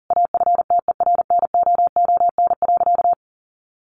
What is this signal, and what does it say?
Morse code, heard as a keyed tone, about 40 wpm, 700 Hz. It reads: AFTERNOON.
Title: Sentences from Top 1500 Words 40wpm